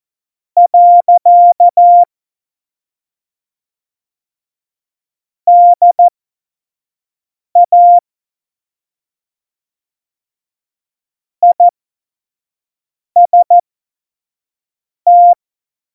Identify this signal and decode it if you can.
Morse code, heard as a keyed tone, about 14 wpm, spaced as for 5 wpm, 700 Hz. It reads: . DA IST
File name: Morsekurs / Witze / Witz 11. September 2018, 25 F70